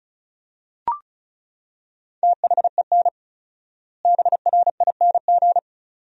Morse code, heard as a keyed tone, about 34 wpm, 700 Hz. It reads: THEN BRING
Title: 4x Word-Spacing - Sentences from Top 500 Words - 35wpm